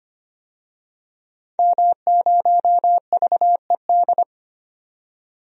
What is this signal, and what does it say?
Morse code, heard as a keyed tone, about 25 wpm, 700 Hz. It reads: M0VED
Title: DXCC-100 Call Signs - 5 Characters - 25wpm